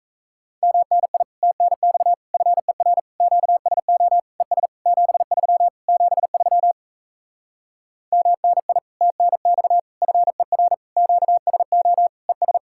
Morse code, heard as a keyed tone, about 37 wpm, 700 Hz. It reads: MNI TNX FER QSO ES 73 73 MNI TNX FER QSO ES